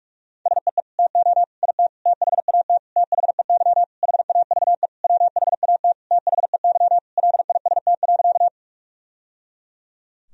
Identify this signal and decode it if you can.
Morse code, heard as a keyed tone, about 41 wpm, 700 Hz. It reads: SEE TO IT THAT THEY HAVE WHAT THEY LIST.
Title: Sentences from Top 1000 Words - Review Part 2 - 45wpm